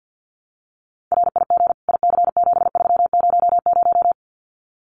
Morse code, heard as a keyed tone, about 50 wpm, 700 Hz. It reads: RIG IC7300
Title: Common QSO Phrases Encore 50wpm